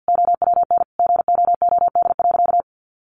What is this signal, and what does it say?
Morse code, heard as a keyed tone, about 50 wpm, 700 Hz. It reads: OWN GOOD.
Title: Sentences from Top 1300 Words 50wpm